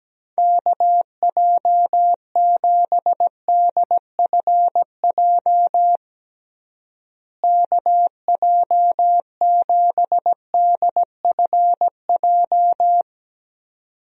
Morse code, heard as a keyed tone, about 17 wpm, 700 Hz. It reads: KJ7DFJ KJ7DFJ
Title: USA Call Signs - 6 Characters - Repeated 2x - 17wpm